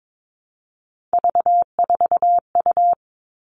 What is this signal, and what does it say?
Morse code, heard as a keyed tone, about 22 wpm, 700 Hz. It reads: V4U
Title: All Cut Numbers - 3-Digit 22wpm